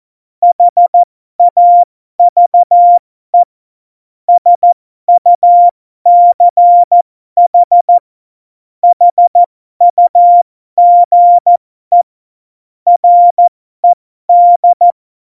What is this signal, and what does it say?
Morse code, heard as a keyed tone, about 14 wpm, spaced as for 12 wpm, 700 Hz. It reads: HAVE SUCH HUGE RED
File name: Morsekurs / Witze